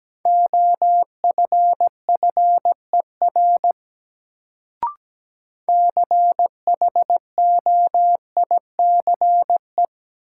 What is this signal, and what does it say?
Morse code, heard as a keyed tone, about 17 wpm, 700 Hz. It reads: OFFER CHOICE
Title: Sentences from Top 900 Words - New Word Drill - 17wpm